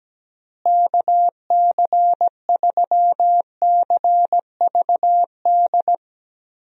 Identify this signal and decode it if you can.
Morse code, heard as a keyed tone, about 17 wpm, 700 Hz. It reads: KC3CVD
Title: USA Call Signs - 6 Characters - Repeated 3x - 17wpm